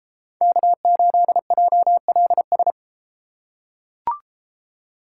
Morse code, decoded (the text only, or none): K8JLS